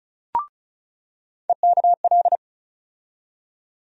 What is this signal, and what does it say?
Morse code, heard as a keyed tone, about 34 wpm, 700 Hz. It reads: EKL
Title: USA Call Sign Postfixes - 3 Letters 35wpm